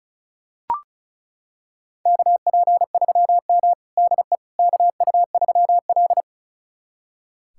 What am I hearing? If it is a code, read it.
Morse code, heard as a keyed tone, about 34 wpm, 700 Hz. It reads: KP3M DE KU3L
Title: USA Callsign DE Callsign - 4 Characters - 35wpm